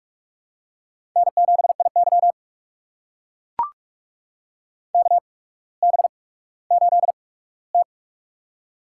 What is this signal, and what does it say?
Morse code, heard as a keyed tone, about 44 wpm, spaced as for 19 wpm, 700 Hz. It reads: N7IY KB8T
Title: USA Call Signs - 4 Characters - 8x Character Spacing - 45wpm